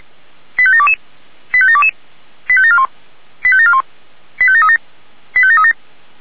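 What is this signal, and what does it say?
Five-tone selective call (ZVEI), heard as quick runs of five tones, tones of about 70 ms. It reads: 85722 85722 85741 85741 85737 85737
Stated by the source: ZVEI five-tone selective call 85729 received on an SDR.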